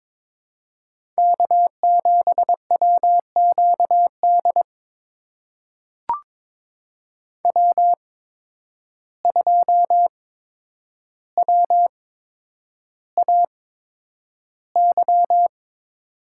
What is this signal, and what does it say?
Morse code, heard as a keyed tone, about 22 wpm, spaced as for 9 wpm, 700 Hz. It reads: K7WQD W2WAY
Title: USA Call Signs - 8x Character Spacing - 22wpm